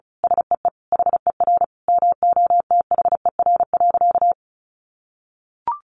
Morse code, heard as a keyed tone, about 34 wpm, 700 Hz. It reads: SEE HER MOTHER.
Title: Sentences from Top 800 Words 35wpm